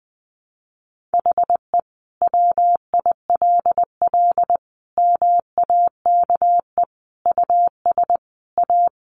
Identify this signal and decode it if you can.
Morse code, heard as a keyed tone, about 20 wpm, 700 Hz. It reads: HE WILL MAKE US A
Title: Sentences from Top 1200 Words 20wpm